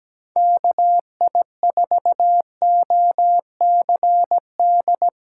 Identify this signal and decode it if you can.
Morse code, heard as a keyed tone, about 17 wpm, 700 Hz. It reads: KI4OCD
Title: USA Call Signs - 6 Characters - 3x Character Spacing - 17wpm